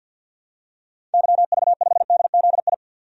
Morse code, heard as a keyed tone, about 46 wpm, 700 Hz. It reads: YV5DZI
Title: DXCC-100 Call Signs - 6 Characters - 50wpm